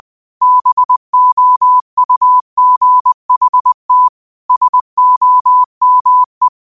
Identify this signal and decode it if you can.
Morse code, heard as a keyed tone, about 20 wpm, 1000 Hz. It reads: BOUGHT SOME